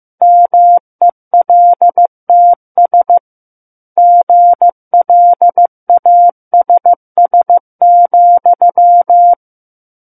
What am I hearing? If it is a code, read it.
Morse code, heard as a keyed tone, about 15 wpm, 700 Hz. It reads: MELTS GLASS,
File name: Chapter 6 - 15 WPM